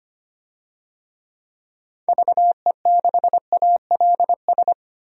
Morse code, heard as a keyed tone, about 25 wpm, 700 Hz. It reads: VE6ALS
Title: DXCC-100 Call Signs - 6 Characters - 25wpm